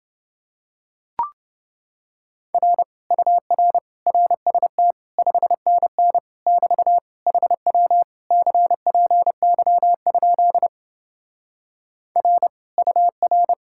Morse code, heard as a keyed tone, about 29 wpm, 700 Hz. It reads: R UR RST 5NN <BT> HW CPY? R UR